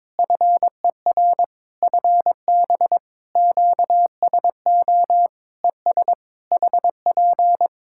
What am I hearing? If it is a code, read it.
Morse code, heard as a keyed tone, about 22 wpm, 700 Hz. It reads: FER FB QSO ES HP
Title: K1USN SST QSO Practice 22wpm